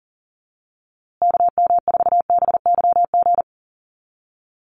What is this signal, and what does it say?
Morse code, heard as a keyed tone, about 39 wpm, 700 Hz. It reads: KM4BYG
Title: USA Call Signs Repeated 3x 40wpm